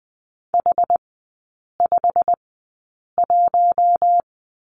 Morse code, heard as a keyed tone, about 20 wpm, spaced as for 8 wpm, 700 Hz. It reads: H51
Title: 145 Lesson 24e - Letter y - Three Characters 20wpm